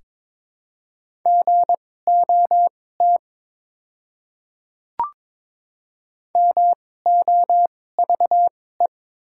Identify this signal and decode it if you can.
Morse code, heard as a keyed tone, about 22 wpm, spaced as for 16 wpm, 700 Hz. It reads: GOT MOVE
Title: 2x Character Spacing - Top 500 Words - 22wpm